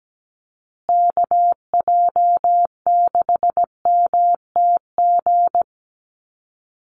Morse code, heard as a keyed tone, about 17 wpm, 700 Hz. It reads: KJ6MTG
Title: USA Call Signs Repeated 3x 17wpm